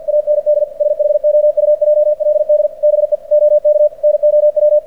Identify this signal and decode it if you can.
Morse code, heard as a keyed tone, about 48 wpm, 600 Hz. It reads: AND SHOUJLD BE OK NOW
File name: archived / CRSnet